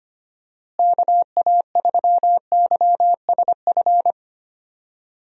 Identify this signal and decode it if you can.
Morse code, heard as a keyed tone, about 25 wpm, 700 Hz. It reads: KA3YSF
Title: USA Call Signs - 6 Characters - Repeated 3x - 25wpm